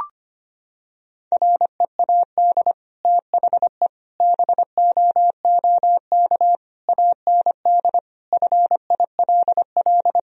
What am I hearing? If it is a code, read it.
Morse code, heard as a keyed tone, about 25 wpm, 700 Hz. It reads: READ THE BOOK AND FILL
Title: Sentences from Top 1000 Words - Review Part 3 - 25wpm